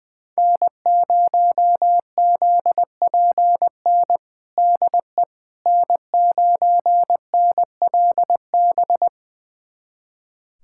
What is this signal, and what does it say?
Morse code, heard as a keyed tone, about 20 wpm, 700 Hz. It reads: N0ZPN DE N9NLB
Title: USA Callsign DE Callsign 20wpm